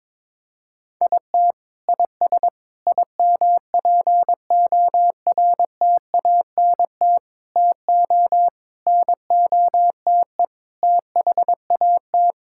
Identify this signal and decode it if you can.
Morse code, heard as a keyed tone, about 21 wpm, 700 Hz. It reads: IT IS IMPORTANT TO NOTE THAT
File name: practice / 22wpm / Sets of 6 Words - 22wpm